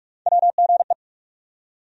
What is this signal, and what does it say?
Morse code, heard as a keyed tone, about 46 wpm, 700 Hz. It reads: WGE
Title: USA Call Sign Postfixes - 3 Letters 45wpm